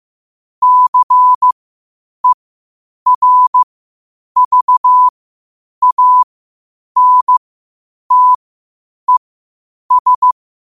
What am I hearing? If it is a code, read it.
Morse code, heard as a keyed tone, about 15 wpm, spaced as for 8 wpm, 1000 Hz. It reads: CERVANTES
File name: qotdfiles / 8wpm